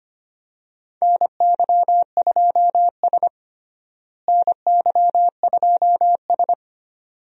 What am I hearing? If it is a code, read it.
Morse code, heard as a keyed tone, about 25 wpm, 700 Hz. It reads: NY2S NY2S
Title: USA Call Signs - 4 Characters - Repeated 2x - 25wpm